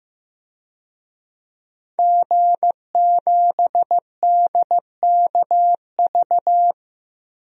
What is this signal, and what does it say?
Morse code, heard as a keyed tone, about 16 wpm, 700 Hz. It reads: G7DKV
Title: DXCC-100 Call Signs - 5 Characters - 15wpm